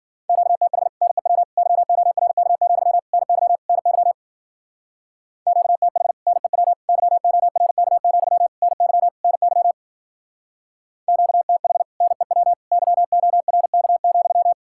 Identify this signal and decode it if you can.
Morse code, heard as a keyed tone, about 50 wpm, 700 Hz. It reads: QTH NEW YORK, NY NY QTH NEW YORK, NY NY QTH NEW YORK,
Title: Speed-Racing Common QSO Phrases - 40wpm